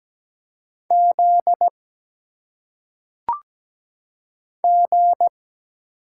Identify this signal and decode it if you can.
Morse code, heard as a keyed tone, about 17 wpm, 700 Hz. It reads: Z G
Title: Single Letters - Phonetic Pronunciation 17wpm